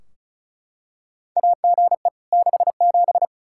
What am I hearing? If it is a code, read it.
Morse code, heard as a keyed tone, about 36 wpm, 700 Hz. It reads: AGE 67
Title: K1USN SST QSO Practice 35wpm